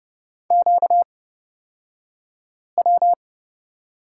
Morse code, heard as a keyed tone, about 30 wpm, spaced as for 5 wpm, 700 Hz. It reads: QW
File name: Lesson 38d - Letter q - Two Characters 30wpm